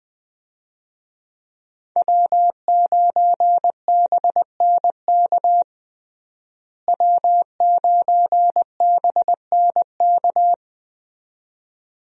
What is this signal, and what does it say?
Morse code, heard as a keyed tone, about 20 wpm, 700 Hz. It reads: W9BNK W9BNK
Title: USA Call Signs - 5 Characters - Repeated 2x - 20wpm